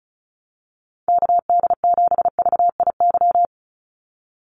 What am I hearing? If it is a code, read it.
Morse code, heard as a keyed tone, about 34 wpm, 700 Hz. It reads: KD7VIY
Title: USA Call Signs 35wpm